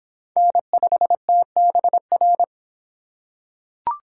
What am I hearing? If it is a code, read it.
Morse code, heard as a keyed tone, about 23 wpm, 700 Hz. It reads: N5TBR